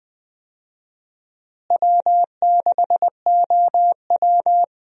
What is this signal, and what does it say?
Morse code, heard as a keyed tone, about 20 wpm, 700 Hz. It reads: W6OW
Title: USA Call Signs - 4 Characters - Repeated 3x - 20wpm